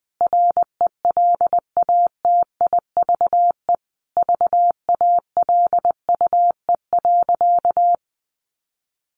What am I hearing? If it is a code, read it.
Morse code, heard as a keyed tone, about 20 wpm, 700 Hz. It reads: RELATIVE VALUE.
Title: Sentences from Top 1400 Words 20wpm